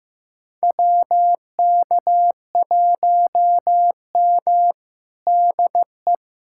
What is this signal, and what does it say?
Morse code, heard as a keyed tone, about 15 wpm, 700 Hz. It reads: WK1M DE
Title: USA Callsign DE Callsign - 4 Characters - 15wpm